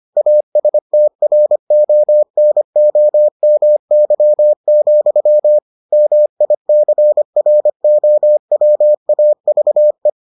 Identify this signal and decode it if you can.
Morse code, heard as a keyed tone, about 25 wpm, 600 Hz. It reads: ASTRONOMY, MICROWAVE